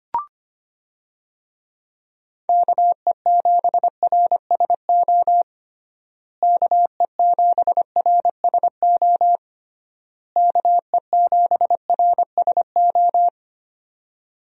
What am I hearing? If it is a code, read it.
Morse code, heard as a keyed tone, about 25 wpm, 700 Hz. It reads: KE7RSO KE7RSO KE7RSO
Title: USA Call Signs - 6 Characters - Repeated 3x - 25wpm